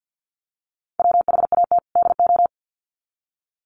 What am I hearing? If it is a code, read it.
Morse code, heard as a keyed tone, about 48 wpm, 700 Hz. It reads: WHAT DO
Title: Sets of 2 Words 50wpm